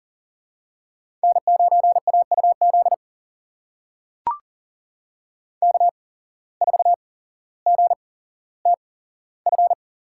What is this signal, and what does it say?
Morse code, heard as a keyed tone, about 39 wpm, spaced as for 17 wpm, 700 Hz. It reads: N9AUZ K4GTF
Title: USA Call Signs - 5 Characters - 8x Character Spacing - 40wpm